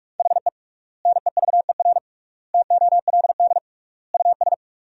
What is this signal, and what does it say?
Morse code, heard as a keyed tone, about 41 wpm, 700 Hz. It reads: HE NEVER TOLD US
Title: 3x Word-Spacing - Sentences from Top 400 Words - 45wpm